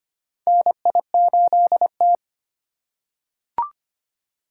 Morse code, heard as a keyed tone, about 25 wpm, 700 Hz. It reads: NI8T
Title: USA Call Signs - 4 Characters - 8x Character Spacing - 25wpm